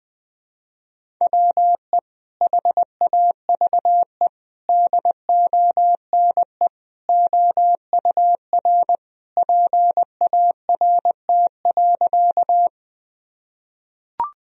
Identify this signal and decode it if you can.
Morse code, heard as a keyed tone, about 20 wpm, 700 Hz. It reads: WE HAVE DONE OUR PART.
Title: Sentences from Top 1000 Words - Review Part 1 - 20wpm